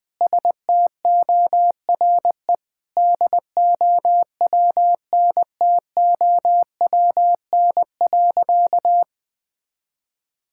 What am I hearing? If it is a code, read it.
Morse code, heard as a keyed tone, about 21 wpm, 700 Hz. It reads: STORE DOWNTOWN.
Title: Sentences with 6-Word Phrases - 20wpm